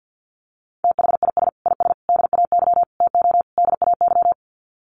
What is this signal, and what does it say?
Morse code, heard as a keyed tone, about 50 wpm, 700 Hz. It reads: THIS IS DAY TO DAY
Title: Sets of 5 Words Encore 50wpm